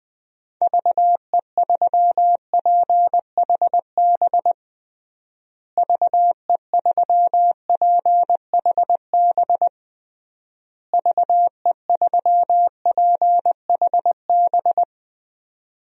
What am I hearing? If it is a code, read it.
Morse code, heard as a keyed tone, about 20 wpm, 700 Hz. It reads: VE3PHB VE3PHB VE3PHB
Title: DXCC-100 Call Signs - 6 Characters - Repeated 3x - 20wpm